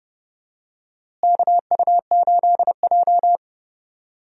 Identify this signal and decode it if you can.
Morse code, heard as a keyed tone, about 32 wpm, 700 Hz. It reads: KU8J
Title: USA Call Signs - 4 Characters - Repeated 2x - 30wpm